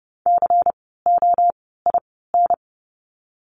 Morse code, heard as a keyed tone, about 30 wpm, 700 Hz. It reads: C O I N
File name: Lesson 16f - Letter c - Words 30wpm